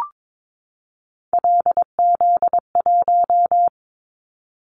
Morse code, heard as a keyed tone, about 22 wpm, 700 Hz. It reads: LZ1
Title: Callsign Prefixes for Top 100 DXCC - 22wpm